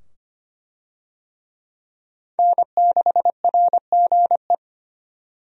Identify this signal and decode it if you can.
Morse code, heard as a keyed tone, about 25 wpm, 700 Hz. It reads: N6RGE
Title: USA Call Signs - 5 Characters - 25wpm